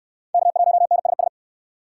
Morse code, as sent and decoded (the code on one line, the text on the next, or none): -.- .---- -. ... ...
K1NSS